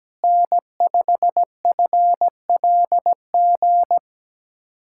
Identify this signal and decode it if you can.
Morse code, heard as a keyed tone, about 17 wpm, 700 Hz. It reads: N5FLG
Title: USA Callsign DE Callsign - 5 Characters - 17wpm